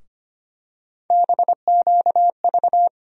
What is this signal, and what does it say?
Morse code, heard as a keyed tone, about 25 wpm, 700 Hz. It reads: BQV
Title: USA Call Sign Postfixes - 3 Letters 25wpm